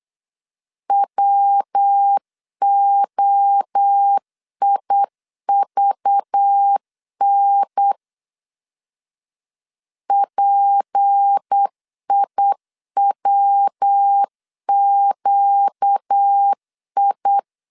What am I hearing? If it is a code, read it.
Morse code, heard as a keyed tone, about 8 wpm, 800 Hz. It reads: WOIVN PIWQI
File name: CW / DET / 1 6 1500C S B 8wpm 5142